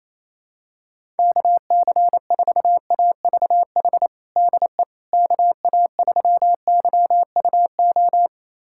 Morse code, heard as a keyed tone, about 28 wpm, 700 Hz. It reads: KC4AVH DE KA3YUO
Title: USA Callsign DE Callsign - 6 Characters - 28wpm